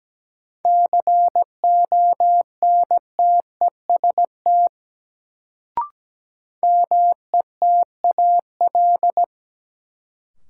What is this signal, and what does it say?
Morse code, heard as a keyed tone, about 17 wpm, 700 Hz. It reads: CONTEST METAL
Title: Sentences from Top 1600 Words - New Word Drill Alt - 17wpm